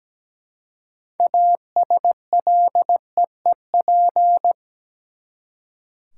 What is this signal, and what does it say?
Morse code, heard as a keyed tone, about 17 wpm, 700 Hz. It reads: ASLEEP
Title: Six Letter Words - 17wpm